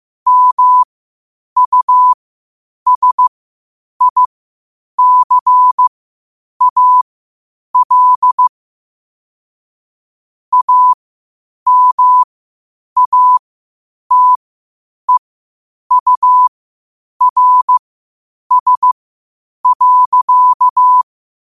Morse code, heard as a keyed tone, about 15 wpm, spaced as for 8 wpm, 1000 Hz. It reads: MUSICAL AMATEURS.
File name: qotdfiles / 8wpm